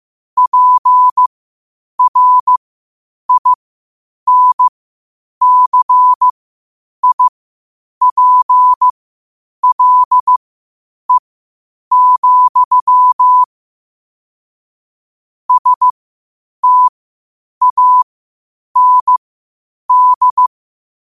Morse code, heard as a keyed tone, about 15 wpm, spaced as for 8 wpm, 1000 Hz. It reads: PRINCIPLE, STAND
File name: qotdfiles / 8wpm